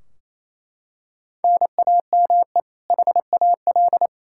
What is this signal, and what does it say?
Morse code, heard as a keyed tone, about 28 wpm, 700 Hz. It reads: NAME HAL
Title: Practice QSO - Exchange 1 - Beginning - 28wpm